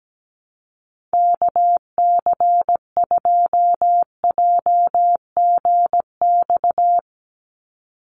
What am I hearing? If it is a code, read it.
Morse code, heard as a keyed tone, about 17 wpm, 700 Hz. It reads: KC2JGX
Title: USA Call Signs 17wpm